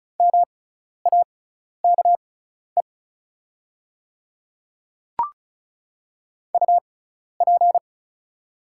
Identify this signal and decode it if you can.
Morse code, heard as a keyed tone, about 36 wpm, spaced as for 12 wpm, 690 Hz. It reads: MAKE UP
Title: 6x Character Spacing - Top 500 Words - 35wpm